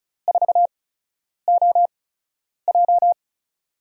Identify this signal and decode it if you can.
Morse code, heard as a keyed tone, about 36 wpm, spaced as for 10 wpm, 700 Hz. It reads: 4OJ
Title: USA Call Signs - 4 Characters - 8x Character Spacing - 35wpm